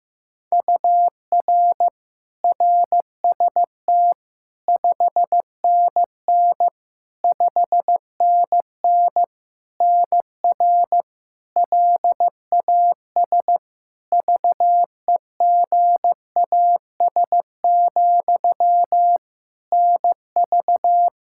Morse code, heard as a keyed tone, about 15 wpm, 700 Hz. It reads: UR RST 5NN 5NN NR LAS VEGAS, NV
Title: K1USN SST QSO Practice 15wpm